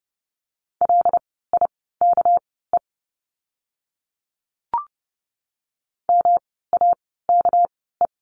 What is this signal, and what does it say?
Morse code, heard as a keyed tone, about 30 wpm, spaced as for 17 wpm, 700 Hz. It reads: LIKE MAKE
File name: course / 30:10wpm / 237 Lesson 38f - Letter q - Words 30wpm